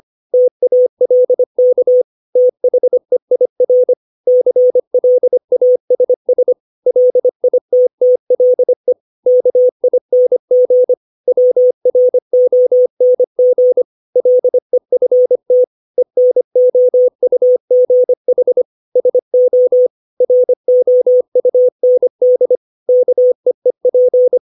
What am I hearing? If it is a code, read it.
Morse code, heard as a keyed tone, about 25 wpm, 500 Hz. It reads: TALK THEIR CLASS LITTLE KING WRONG LEFT ENOUGH SO ROUND KEEP
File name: 632 words / 632 Words 25WPM